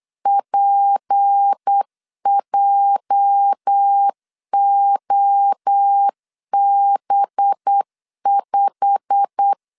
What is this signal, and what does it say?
Morse code, heard as a keyed tone, about 8 wpm, 800 Hz. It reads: PJOB5
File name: LEZIONI / 8 YJ45 / 8 YJ45 8wpm 3214